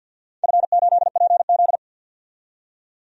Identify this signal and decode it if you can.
Morse code, heard as a keyed tone, about 49 wpm, 700 Hz. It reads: F8PZ
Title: DXCC-100 Call Signs - 4 Characters - 50wpm